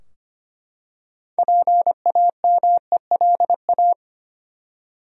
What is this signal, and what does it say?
Morse code, heard as a keyed tone, about 25 wpm, 700 Hz. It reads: PAMELA
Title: USA Common Names - 25wpm